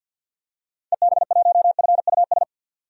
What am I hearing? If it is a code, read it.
Morse code, heard as a keyed tone, about 46 wpm, 700 Hz. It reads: EB1FUS
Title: DXCC-100 Call Signs - 6 Characters - 50wpm